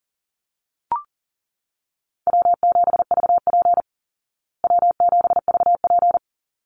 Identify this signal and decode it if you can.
Morse code, heard as a keyed tone, about 40 wpm, 700 Hz. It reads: W7VP W7VP
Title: Call Signs Repeated 3x - 40wpm